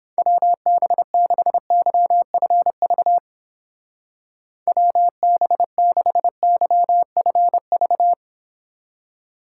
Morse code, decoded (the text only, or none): WB6YFV WB6YFV